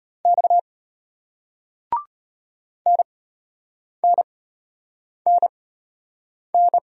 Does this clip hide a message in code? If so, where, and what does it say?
Yes: Morse code from 0.2 to 6.8 s, heard as a keyed tone, about 33 wpm, spaced as for 8 wpm, 700 Hz. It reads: X NNNN